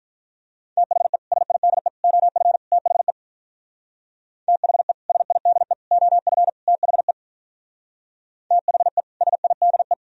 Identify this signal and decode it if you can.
Morse code, heard as a keyed tone, about 43 wpm, 700 Hz. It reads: THE SIDE OF THE THE SIDE OF THE THE SIDE
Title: Speed-Racing - Sets of 4 Words 35wpm